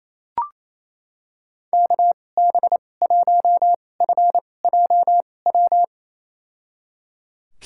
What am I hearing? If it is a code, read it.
Morse code, heard as a keyed tone, about 28 wpm, spaced as for 20 wpm, 700 Hz. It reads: KB1FJW
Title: USA Call Signs - 6 Characters - 2x Character Spacing - 28wpm